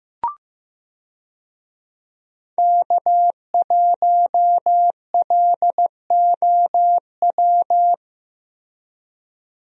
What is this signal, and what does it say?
Morse code, heard as a keyed tone, about 15 wpm, 700 Hz. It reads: K1LOW
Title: USA Call Signs - 5 Characters - Repeated 3x - 15wpm